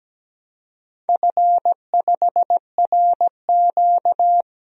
Morse code, heard as a keyed tone, about 17 wpm, 700 Hz. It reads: F5RQ
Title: DXCC-100 Call Signs - 4 Characters - Repeated 3x - 17wpm